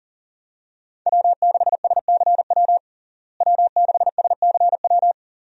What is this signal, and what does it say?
Morse code, heard as a keyed tone, about 40 wpm, 700 Hz. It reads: W6SCW W6SCW
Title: USA Call Signs - 5 Characters - Repeated 2x - 40wpm